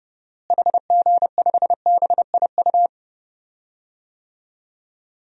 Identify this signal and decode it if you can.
Morse code, heard as a keyed tone, about 32 wpm, 700 Hz. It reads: HG5BIU
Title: DXCC-100 Call Signs - 6 Characters - Repeated 3x - 30wpm